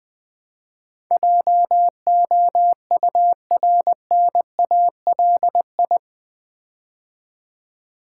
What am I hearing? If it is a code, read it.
Morse code, heard as a keyed tone, about 20 wpm, 700 Hz. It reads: JOURNALI
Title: Build A Word - 4 Letters At A Time - 10-Letter Words - 20wpm